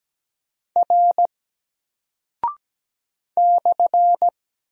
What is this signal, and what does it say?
Morse code, heard as a keyed tone, about 17 wpm, 700 Hz. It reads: R /
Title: Single Character (Letter:Number:Punctuation:Prosign) Alt 17wpm